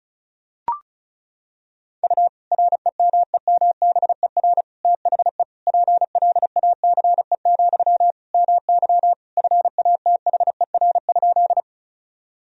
Morse code, heard as a keyed tone, about 36 wpm, 700 Hz. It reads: U REMEMBER THE PLACE, MY FATHER?
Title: Sentences from Top 400 Words and Abbreviations 35wpm